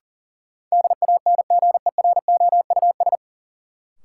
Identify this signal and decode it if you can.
Morse code, heard as a keyed tone, about 40 wpm, 700 Hz. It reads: DANGEROUS
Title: Sentences from Top 1200 Words - New Word Drill - 40wpm